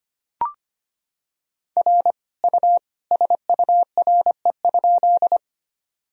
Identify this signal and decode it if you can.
Morse code, heard as a keyed tone, about 25 wpm, 700 Hz. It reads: R U SURE?
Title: Sentences from Top 300 Words and Abbreviations 25wpm